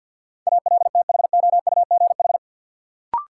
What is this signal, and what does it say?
Morse code, heard as a keyed tone, about 52 wpm, 700 Hz. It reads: ALTHOUGH
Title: Sentences from Top 600 Words - New Word Drill - 50wpm